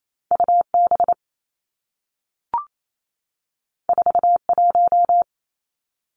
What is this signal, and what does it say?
Morse code, heard as a keyed tone, about 28 wpm, 700 Hz. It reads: U B 4 1
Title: Two Challenging Letter-Number Combinations 28wpm